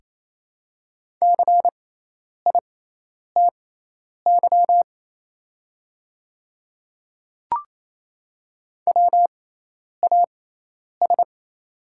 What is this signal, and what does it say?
Morse code, heard as a keyed tone, about 28 wpm, spaced as for 10 wpm, 700 Hz. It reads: CITY WAS
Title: 6x Character Spacing - Top 300 Words - 28wpm